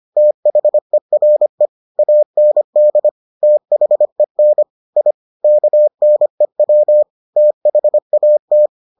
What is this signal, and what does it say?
Morse code, heard as a keyed tone, about 25 wpm, 600 Hz. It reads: THERE AND THEN I KNEW THAT